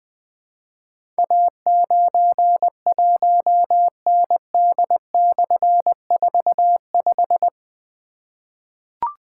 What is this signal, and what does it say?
Morse code, heard as a keyed tone, about 20 wpm, 700 Hz. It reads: A91ND/45
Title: Difficult Call Signs 20wpm